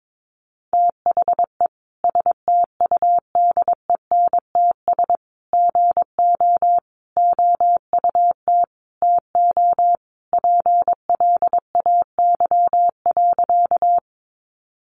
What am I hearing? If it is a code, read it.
Morse code, heard as a keyed tone, about 22 wpm, 700 Hz. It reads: THE STUDENTS GO OUT TO PLAY.
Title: Easy Phrases 22wpm